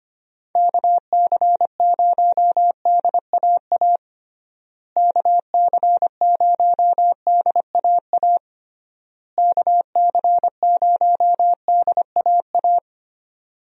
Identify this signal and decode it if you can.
Morse code, heard as a keyed tone, about 25 wpm, 700 Hz. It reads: KC0DAA KC0DAA KC0DAA
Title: USA Call Signs - 6 Characters - Repeated 3x - 25wpm